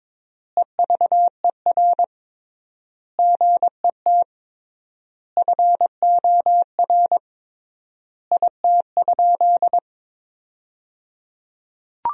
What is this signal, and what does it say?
Morse code, heard as a keyed tone, about 22 wpm, 700 Hz. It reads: EVER GET FOR IT?
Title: 3x Word-Spacing - Sentences from Top 400 Words - 22wpm